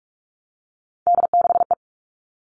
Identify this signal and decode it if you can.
Morse code, heard as a keyed tone, about 44 wpm, 700 Hz. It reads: D6E
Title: All Cut Numbers - 3-Digit 45wpm